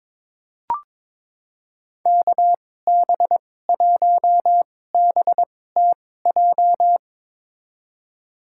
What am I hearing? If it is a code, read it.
Morse code, heard as a keyed tone, about 22 wpm, spaced as for 16 wpm, 700 Hz. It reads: KB1BTJ
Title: USA Call Signs - 6 Characters - 2x Character Spacing - 22wpm